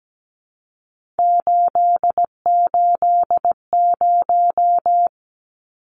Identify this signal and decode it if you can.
Morse code, heard as a keyed tone, about 17 wpm, 700 Hz. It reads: 880
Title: Numbers - 3-Digit 17wpm